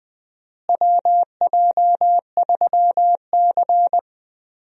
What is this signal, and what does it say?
Morse code, heard as a keyed tone, about 20 wpm, 700 Hz. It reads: WJ3C
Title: USA Call Signs - 4 Characters - Repeated 2x - 20wpm